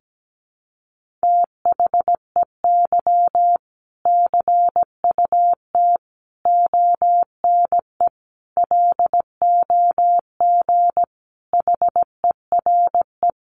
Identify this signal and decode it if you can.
Morse code, heard as a keyed tone, about 17 wpm, 700 Hz. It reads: THEY CUT ONE LOG HERE
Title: Sentences from Top 1400 Words 17wpm